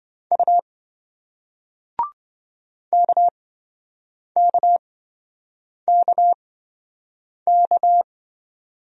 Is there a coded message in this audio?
Yes: Morse code from 0.3 to 8.0 s, heard as a keyed tone, about 27 wpm, spaced as for 7 wpm, 700 Hz. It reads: U KKKK